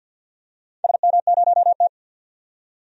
Morse code, heard as a keyed tone, about 52 wpm, 700 Hz. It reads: SM0T
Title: DXCC-100 Call Signs - 4 Characters - 50wpm